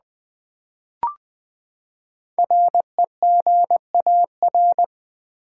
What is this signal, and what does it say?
Morse code, heard as a keyed tone, about 20 wpm, 700 Hz. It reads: REGAR
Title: Build A Word - 5 Letters At A Time - 10-Letter Words - 20wpm